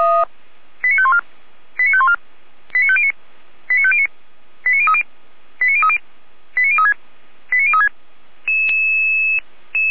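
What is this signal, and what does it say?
Five-tone selective call (ZVEI), heard as quick runs of five tones, tones of about 70 ms. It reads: 89514 89514 89509 89509 89030 89030 89036 89036
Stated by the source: ZVEI five-tone selective call received by SDR, from feuerwehr-orlamuende.org fire brigade server.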